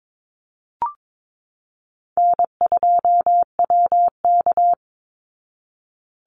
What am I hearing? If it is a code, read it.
Morse code, heard as a keyed tone, about 22 wpm, 700 Hz. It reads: N2WK
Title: Easy Call Signs Repeated 2x - 22wpm